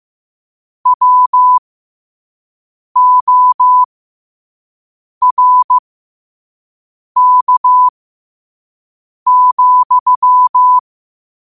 Morse code, heard as a keyed tone, about 15 wpm, spaced as for 5 wpm, 1000 Hz. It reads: WORK,
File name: qotdfiles / 5wpm